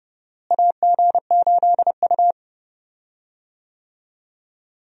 Morse code, heard as a keyed tone, about 32 wpm, 700 Hz. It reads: AG8U
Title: USA Call Signs - 4 Characters - Repeated 3x - 30wpm